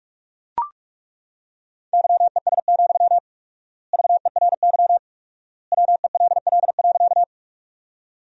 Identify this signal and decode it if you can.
Morse code, heard as a keyed tone, about 42 wpm, 700 Hz. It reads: YES, VERY WELL.
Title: 4x Word-Spacing - Sentences from Top 500 Words - 45wpm